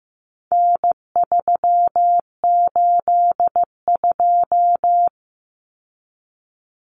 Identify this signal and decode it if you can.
Morse code, heard as a keyed tone, about 15 wpm, 700 Hz. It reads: N382
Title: Common Cut Numbers - 4-Digit 15wpm